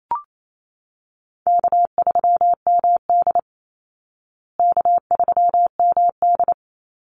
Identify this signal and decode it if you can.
Morse code, heard as a keyed tone, about 28 wpm, 700 Hz. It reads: K3MD K3MD
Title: Easy Call Signs Repeated 3x - 28wpm